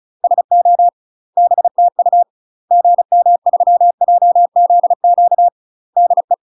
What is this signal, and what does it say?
Morse code, heard as a keyed tone, about 34 wpm, 700 Hz. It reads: SO BTU GM3JZQ DE
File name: UK MASTER QSOs TRAINING / 35WPM / UK MASTER QSOs 0000